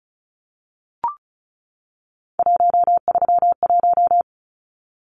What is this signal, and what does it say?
Morse code, heard as a keyed tone, about 34 wpm, 700 Hz. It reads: 131
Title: Numbers - 3-Digit 35wpm